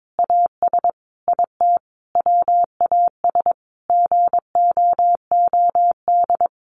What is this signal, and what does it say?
Morse code, heard as a keyed tone, about 22 wpm, 700 Hz. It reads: AS IT WAS GOOD
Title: Sentences from Top 600 Words 22wpm